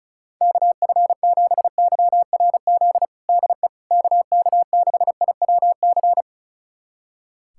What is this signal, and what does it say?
Morse code, heard as a keyed tone, about 36 wpm, 700 Hz. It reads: KF7YRZ DE KK6IWC
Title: USA Callsign DE Callsign - 6 Characters - 35wpm